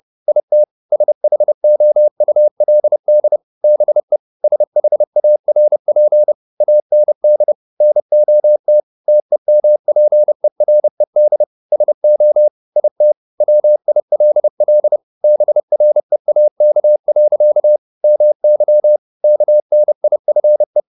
Morse code, heard as a keyed tone, about 30 wpm, 600 Hz. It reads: IT SHOULD BE SHARP AND NOT TEMPERED SO IT WILL BREAK. MY KNIFE